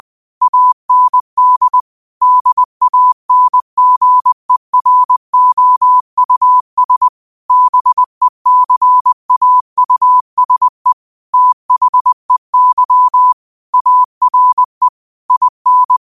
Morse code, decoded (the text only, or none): AND DANGEROUS BECAUSE THEY ARE IN